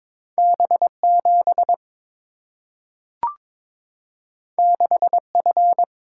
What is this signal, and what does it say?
Morse code, heard as a keyed tone, about 22 wpm, 690 Hz. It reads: B7 6F
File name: practice / 22wpm / Two Letter-Number Combinations - 22wpm